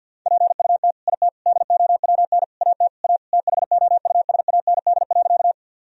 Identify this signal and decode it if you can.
Morse code, heard as a keyed tone, about 50 wpm, 700 Hz. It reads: PUT IT DOWN AT A THOUSAND.
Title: Sentences from Top 1000 Words - Review Part 3 - 50wpm